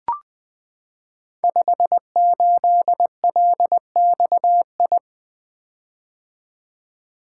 Morse code, heard as a keyed tone, about 20 wpm, 700 Hz. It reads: 58LXI
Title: Code Groups - 5 Letter:Numbers - 20wpm